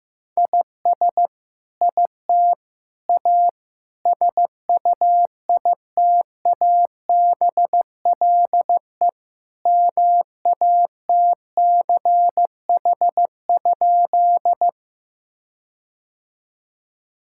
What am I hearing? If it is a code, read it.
Morse code, heard as a keyed tone, about 15 wpm, 700 Hz. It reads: IS IT A SUITABLE MATCH?
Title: Sentences from Top 1600 Words 15wpm